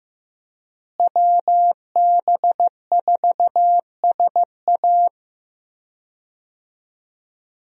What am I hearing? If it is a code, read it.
Morse code, heard as a keyed tone, about 16 wpm, 700 Hz. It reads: WB4SA
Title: USA Call Signs - 5 Characters - Repeated 2x - 15wpm